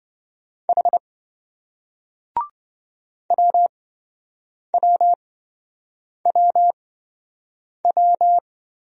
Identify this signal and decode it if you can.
Morse code, heard as a keyed tone, about 28 wpm, spaced as for 7 wpm, 700 Hz. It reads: H WWWW